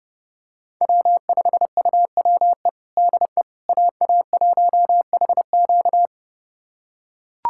Morse code, heard as a keyed tone, about 32 wpm, 700 Hz. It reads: W5UWE DE AA1HQ
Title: USA Callsign DE Callsign - 5 Characters - 30wpm